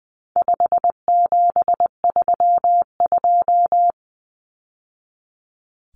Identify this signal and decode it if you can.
Morse code, heard as a keyed tone, about 19 wpm, 700 Hz. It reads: 5732
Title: Numbers - 4-Digit 20wpm